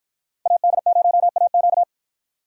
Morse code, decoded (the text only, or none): AD0AQ